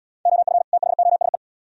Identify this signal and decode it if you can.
Morse code, heard as a keyed tone, about 71 wpm, 700 Hz. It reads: OF THOSE